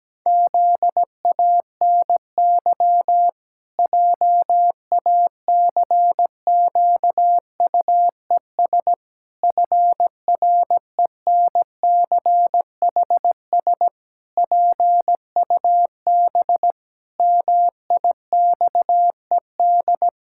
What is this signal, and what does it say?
Morse code, heard as a keyed tone, about 17 wpm, 700 Hz. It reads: ZANY JACQUES FRENCHS PUB MIXED
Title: Pangram Challenge v2 - 17wpm